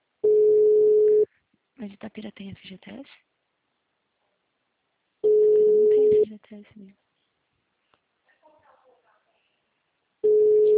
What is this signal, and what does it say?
Ringback tone (Europe, 425 Hz), with speech.